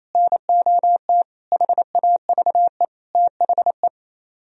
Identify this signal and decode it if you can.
Morse code, heard as a keyed tone, about 35 wpm, 700 Hz. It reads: NOT HAVE THE
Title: Speed-Racing - Sets of 5 Words 28wpm